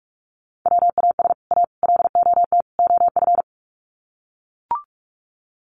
Morse code, heard as a keyed tone, about 44 wpm, 700 Hz. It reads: WAS A LOT OF
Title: Sets of 4 Words 45wpm